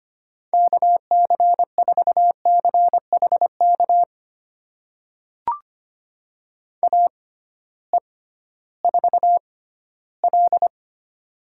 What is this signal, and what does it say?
Morse code, heard as a keyed tone, about 25 wpm, spaced as for 14 wpm, 700 Hz. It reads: KC4CHK AE4L
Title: USA Call Signs - 6x Character Spacing - 25wpm